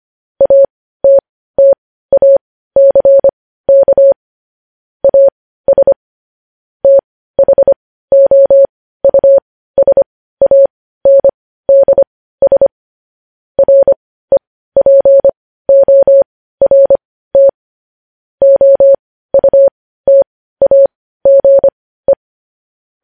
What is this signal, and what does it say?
Morse code, heard as a keyed tone, about 25 wpm, spaced as for 15 wpm, 550 Hz. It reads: ATTACK AS THOUSANDS REPORT OUTAGE
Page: news headlines in morse code at 15 wpm